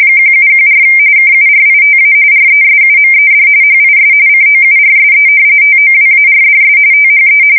RTTY, about 45.45 baud. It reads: 1714 KURITA RESUMED HIS COURSE TOWARDS THE
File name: fsk170 / battleshipnj